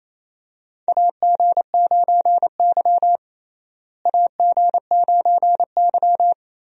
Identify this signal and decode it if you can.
Morse code, heard as a keyed tone, about 28 wpm, 700 Hz. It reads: AG9Y AG9Y
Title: USA Call Signs - 4 Characters - Repeated 2x - 28wpm